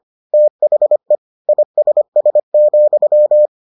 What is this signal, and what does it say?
Morse code, heard as a keyed tone, about 25 wpm, 600 Hz. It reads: THE ISS,